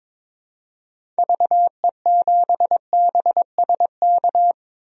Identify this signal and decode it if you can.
Morse code, heard as a keyed tone, about 22 wpm, 700 Hz. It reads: VE7BSK
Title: DXCC-100 Call Signs - 6 Characters - 22wpm